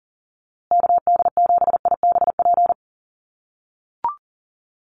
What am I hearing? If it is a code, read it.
Morse code, heard as a keyed tone, about 40 wpm, 700 Hz. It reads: KD7IBP
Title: USA Call Signs Repeated 3x 40wpm